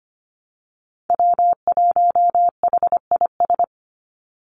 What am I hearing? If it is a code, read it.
Morse code, heard as a keyed tone, about 25 wpm, 700 Hz. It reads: W1HIS
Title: Call Signs - 25wpm